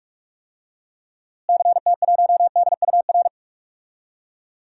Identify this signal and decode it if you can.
Morse code, heard as a keyed tone, about 41 wpm, 700 Hz. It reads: CT1DUR
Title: DXCC-100 Call Signs - 6 Characters - Repeated 3x - 45wpm